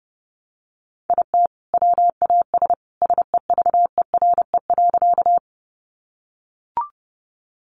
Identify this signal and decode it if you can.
Morse code, heard as a keyed tone, about 30 wpm, 700 Hz. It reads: IT WAS SEVERE.
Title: Sentences from Top 1400 Words 30wpm